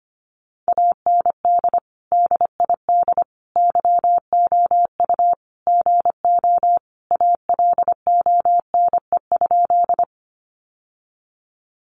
Morse code, heard as a keyed tone, about 25 wpm, 700 Hz. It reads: AND DID YOU GO ALONE?
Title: Sentences from Top 1000 Words 25wpm